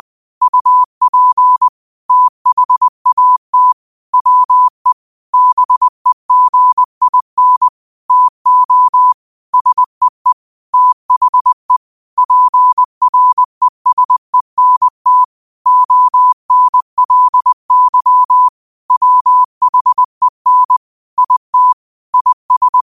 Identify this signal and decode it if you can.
Morse code, heard as a keyed tone, about 20 wpm, 1000 Hz. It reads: UP THAT WE BEGIN TO SEE THE PRESENT ONLY WHEN IT IS
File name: qotdfiles / 20wpm